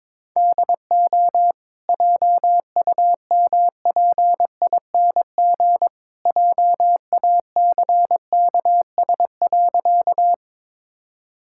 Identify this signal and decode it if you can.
Morse code, heard as a keyed tone, about 22 wpm, 690 Hz. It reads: DO JUMPING JACKS.
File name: practice / 22wpm / Easy Sentences with Hints - 22wpm